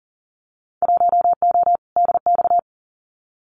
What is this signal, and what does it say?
Morse code, heard as a keyed tone, about 41 wpm, 700 Hz. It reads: 1O DX
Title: Winter Field Day Exchanges 40wpm